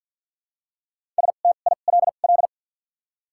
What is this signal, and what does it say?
Morse code, heard as a keyed tone, about 48 wpm, spaced as for 35 wpm, 700 Hz. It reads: STILL
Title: 2x Character Spacing - Top 500 Words - 50wpm